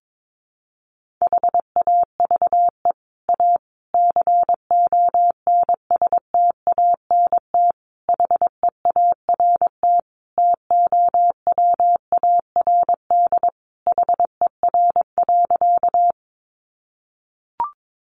Morse code, heard as a keyed tone, about 22 wpm, 700 Hz. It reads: HAVE A CONSTANT HEART TOWARD HER.
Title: Sentences from Top 1500 Words 22wpm